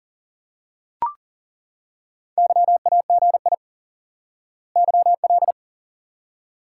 Morse code, heard as a keyed tone, about 40 wpm, 700 Hz. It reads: YAGI YL
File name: IQE Course 3 - Lesson 89 - YAGI - Two Elements - 40wpm